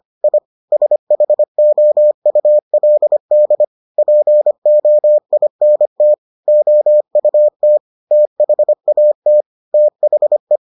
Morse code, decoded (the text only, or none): I SHOULD POINT OUT THAT THE